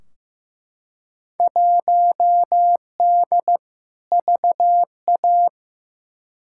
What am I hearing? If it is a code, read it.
Morse code, heard as a keyed tone, about 15 wpm, 700 Hz. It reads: 1D VA
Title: Field Day Exchanges - 15wpm